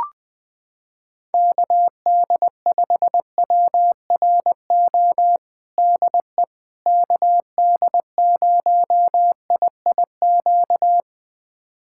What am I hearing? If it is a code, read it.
Morse code, heard as a keyed tone, about 21 wpm, 700 Hz. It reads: KD5WRO DE KD0IIQ
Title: USA Callsign DE Callsign - 6 Characters - 20wpm